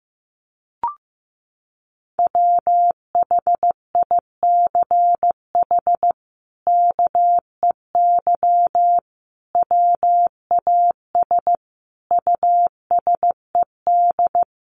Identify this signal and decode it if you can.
Morse code, heard as a keyed tone, about 15 wpm, 700 Hz. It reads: WHICH KEY WAS USED
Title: Sentences from Top 600 Words 15wpm